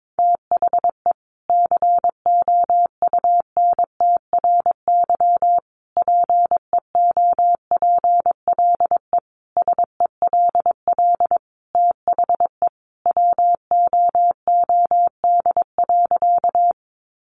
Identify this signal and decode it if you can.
Morse code, heard as a keyed tone, about 22 wpm, 700 Hz. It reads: THE COUNTRY PEOPLE SELL THE WOOD.
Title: Sentences from Top 1100 Words 22wpm